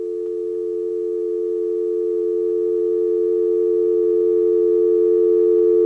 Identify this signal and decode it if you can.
Dial tone (North America), 350 + 440 Hz.